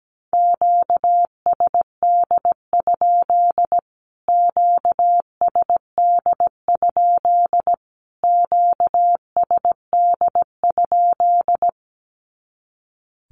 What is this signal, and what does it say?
Morse code, heard as a keyed tone, about 17 wpm, 700 Hz. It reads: QSD? QSD? QSD?
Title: Q-Codes Repeated 3x 17wpm